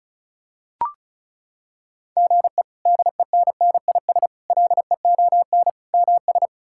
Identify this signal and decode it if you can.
Morse code, heard as a keyed tone, about 36 wpm, 700 Hz. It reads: GE DENNIS LEON MS
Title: K1USN SST QSO Practice - 35wpm